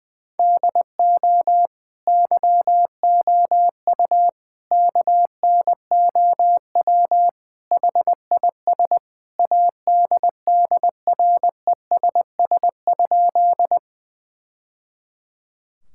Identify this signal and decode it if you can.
Morse code, heard as a keyed tone, about 20 wpm, 700 Hz. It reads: DO YOU KNOW HIS ADDRESS?
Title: Sentences from Top 1000 Words - Review Part 6 - 20wpm